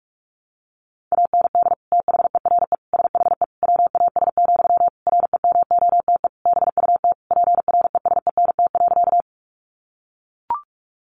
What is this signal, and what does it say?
Morse code, heard as a keyed tone, about 44 wpm, 700 Hz. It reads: AND THERE SHE WAS, REMOTE BUT PRESENT.
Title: Sentences from Top 1300 Words 45wpm